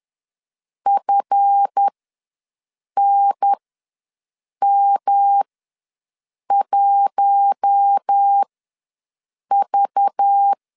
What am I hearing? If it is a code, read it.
Morse code, heard as a keyed tone, about 10 wpm, spaced as for 6 wpm, 800 Hz. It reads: FNM1V
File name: LEZIONI / 7 GF9Z / 7 GF ws 10wpm 3918